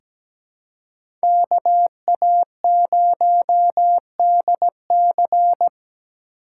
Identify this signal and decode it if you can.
Morse code, heard as a keyed tone, about 17 wpm, 700 Hz. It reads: KA0DC
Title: USA Call Signs - 5 Characters - 3x Character Spacing - 17wpm